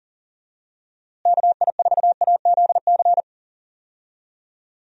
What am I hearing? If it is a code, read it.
Morse code, heard as a keyed tone, about 40 wpm, 700 Hz. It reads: KI4AZC
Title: USA Call Signs - 6 Characters - Repeated 3x - 40wpm